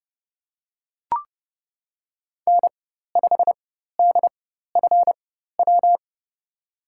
Morse code, heard as a keyed tone, about 30 wpm, spaced as for 14 wpm, 700 Hz. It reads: N5DFW
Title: USA Call Signs - 5 Characters - 4x Character Spacing - 30wpm